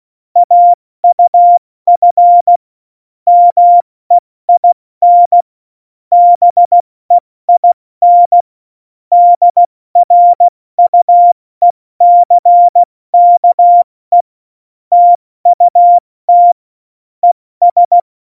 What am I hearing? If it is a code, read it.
Morse code, heard as a keyed tone, about 16 wpm, 700 Hz. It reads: AUF MEIN BEIN DRUECKE TUT ES
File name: Morsekurs / Witze / Witz 09. Januar 2018, 70 F80